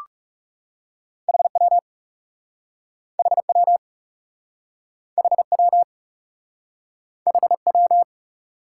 Morse code, heard as a keyed tone, about 40 wpm, 700 Hz. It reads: HW HW HW HW